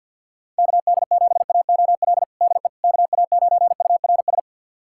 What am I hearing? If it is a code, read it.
Morse code, heard as a keyed tone, about 52 wpm, 700 Hz. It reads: KD7AOL DE KA9URS
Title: USA Callsign DE Callsign 50wpm